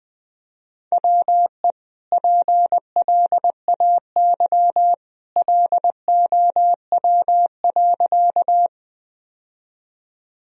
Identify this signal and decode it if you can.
Morse code, heard as a keyed tone, about 21 wpm, 700 Hz. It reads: WE PLAY LOW.
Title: Sentences from Top 1000 Words - Review Part 1 - 20wpm